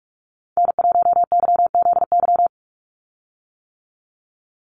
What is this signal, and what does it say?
Morse code, heard as a keyed tone, about 46 wpm, 690 Hz. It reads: N1YZY
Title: USA Call Signs 45wpm